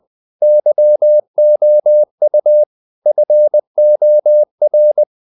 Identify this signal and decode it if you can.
Morse code, heard as a keyed tone, about 20 wpm, 600 Hz. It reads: YOU FOR
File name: the old man and the sea / the old man and the sea part 1 20wpm